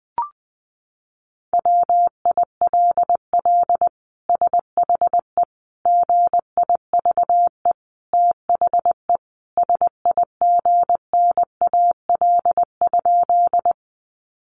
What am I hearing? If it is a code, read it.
Morse code, heard as a keyed tone, about 20 wpm, 700 Hz. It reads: WILL SHE GIVE THE SIGNAL?
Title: Sentences from Top 1400 Words 20wpm